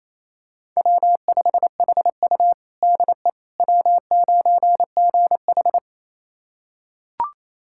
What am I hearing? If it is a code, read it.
Morse code, heard as a keyed tone, about 28 wpm, 700 Hz. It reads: W5HU DE W9GH
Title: USA Callsign DE Callsign - 4 Characters - 28wpm